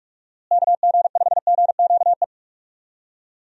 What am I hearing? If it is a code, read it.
Morse code, heard as a keyed tone, about 42 wpm, 700 Hz. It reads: KG5GQE
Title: USA Call Signs - 6 Characters - Repeated 3x - 45wpm